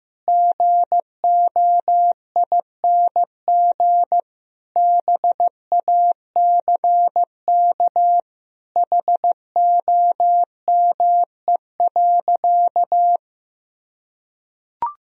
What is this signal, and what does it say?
Morse code, heard as a keyed tone, about 15 wpm, 700 Hz. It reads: GOING BACK HOME.
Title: Sentences from Top 500 Words and Abbreviations 15wpm